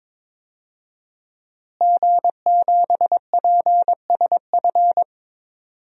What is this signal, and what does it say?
Morse code, heard as a keyed tone, about 22 wpm, 700 Hz. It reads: G7PSF
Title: DXCC-100 Call Signs - 5 Characters - Repeated 3x - 22wpm